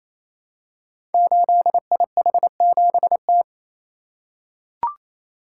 Morse code, heard as a keyed tone, about 27 wpm, 700 Hz. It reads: 8IH7T
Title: Code Groups - 5 Letter:Numbers - 28wpm